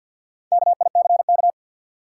Morse code, heard as a keyed tone, about 51 wpm, 700 Hz. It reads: KICK
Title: Sentences from Top 1300 Words - New Word Drill - 50wpm